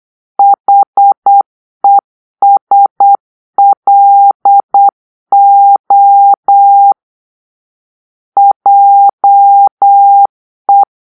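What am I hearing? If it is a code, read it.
Morse code, heard as a keyed tone, about 8 wpm, 800 Hz. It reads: HESLO JE
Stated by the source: Morse-code cipher made by a player team for the Kachna puzzle game.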